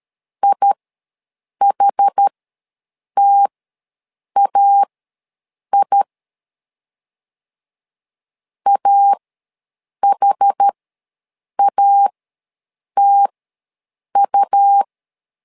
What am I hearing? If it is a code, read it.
Morse code, heard as a keyed tone, about 13 wpm, spaced as for 7 wpm, 800 Hz. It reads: IHTAI AHATU
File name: LEZIONI / 2 UHDN / 2 UH ws 12wpm 1730